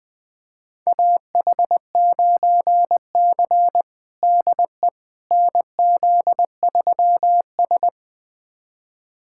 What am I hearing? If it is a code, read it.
Morse code, heard as a keyed tone, about 19 wpm, 700 Hz. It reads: AH9C DE NZ3S
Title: USA Callsign DE Callsign - 4 Characters - 20wpm